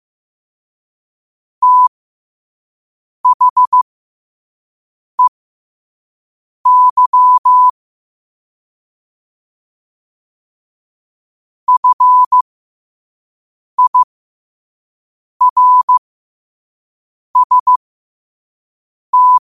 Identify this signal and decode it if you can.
Morse code, heard as a keyed tone, about 15 wpm, spaced as for 5 wpm, 1000 Hz. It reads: THEY FIRST
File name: qotdfiles / 5wpm